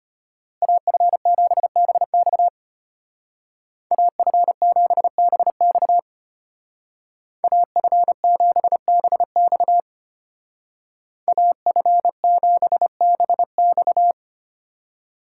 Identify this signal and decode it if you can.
Morse code, heard as a keyed tone, about 30 wpm, 700 Hz. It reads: AF7BX AF7BX AF7BX AF7BX
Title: Speed Racing - USA 2x2 Call Signs 25wpm